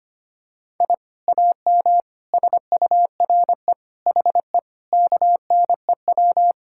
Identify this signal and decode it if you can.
Morse code, heard as a keyed tone, about 25 wpm, 700 Hz. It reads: I AM SURE HE KNEW
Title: Sentences from Top 1000 Words - Review Part 6 - 25wpm